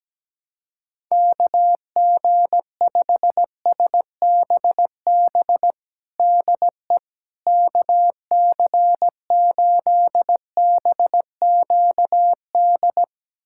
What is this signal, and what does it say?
Morse code, heard as a keyed tone, about 17 wpm, 700 Hz. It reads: KG5SBB DE KC8BQD
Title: USA Callsign DE Callsign - 6 Characters - 17wpm